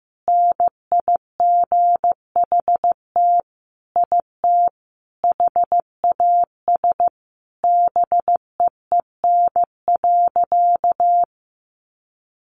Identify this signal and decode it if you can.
Morse code, heard as a keyed tone, about 14 wpm, 700 Hz. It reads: NIGHT IT HAS BEEN.
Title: Sentences from Top 300 Words 15wpm